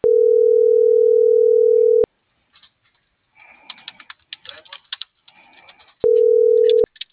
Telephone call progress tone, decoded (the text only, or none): ringback tone